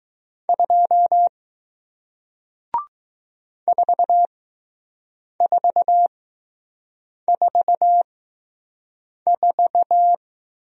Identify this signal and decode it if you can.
Morse code, heard as a keyed tone, about 20 wpm, spaced as for 6 wpm, 700 Hz. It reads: 2 4444